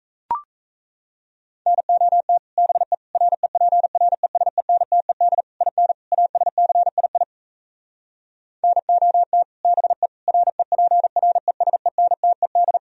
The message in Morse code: NOT BE REPRESENTED IN ASCII NOT BE REPRESENTED

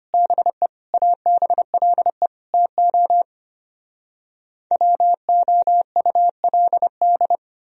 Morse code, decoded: BE ABLE TO WOULD

